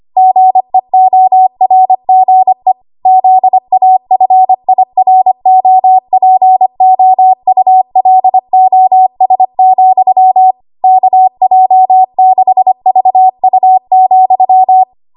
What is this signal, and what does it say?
Morse code, heard as a keyed tone, about 25 wpm, 750 Hz. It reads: GEORGE ZAFIROPOULOS, KJ6VU,
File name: Morse / 25WPM